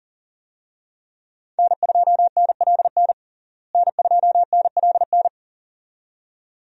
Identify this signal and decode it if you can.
Morse code, heard as a keyed tone, about 40 wpm, 700 Hz. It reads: N2NLN N2NLN
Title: USA Call Signs - 5 Characters - Repeated 2x - 40wpm